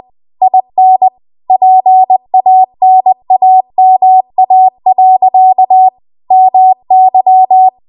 Morse code, heard as a keyed tone, about 20 wpm, 750 Hz. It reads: IN PANAMA. MY
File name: Archive / 20 WPM / 20WPM